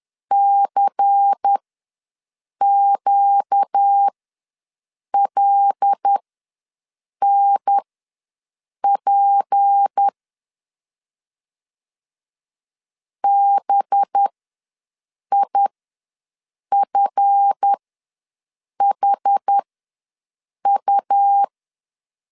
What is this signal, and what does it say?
Morse code, heard as a keyed tone, about 10 wpm, spaced as for 6 wpm, 800 Hz. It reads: CQLNP BIFHU
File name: LEZIONI / 7 GF9Z / 7 GF9 ws 10wpm 4034